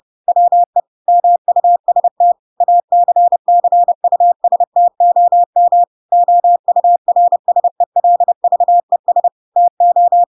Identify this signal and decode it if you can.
Morse code, heard as a keyed tone, about 30 wpm, 700 Hz. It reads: WE MUST ACCUSTOM OURSELVES TO
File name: Pinocchio / Pinocchio ch7 30 30 ik1hge